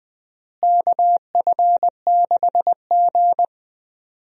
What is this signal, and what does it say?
Morse code, heard as a keyed tone, about 21 wpm, 700 Hz. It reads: KF6G
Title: USA Call Signs - 4 Characters - Repeated 3x - 20wpm